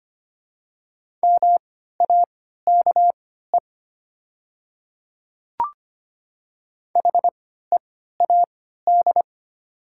Morse code, heard as a keyed tone, about 25 wpm, spaced as for 14 wpm, 690 Hz. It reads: MAKE HEAD
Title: 3x Character Spacing - Top 500 Words - 25wpm